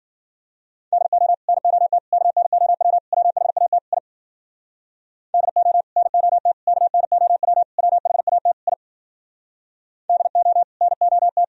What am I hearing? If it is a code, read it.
Morse code, heard as a keyed tone, about 52 wpm, 700 Hz. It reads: DO NOT KNOW WHAT I DO NOT KNOW WHAT I DO NOT